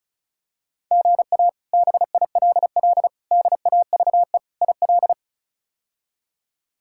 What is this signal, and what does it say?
Morse code, heard as a keyed tone, about 36 wpm, 700 Hz. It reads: GA BILL DAVE IL
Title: K1USN SST QSO Practice - 35wpm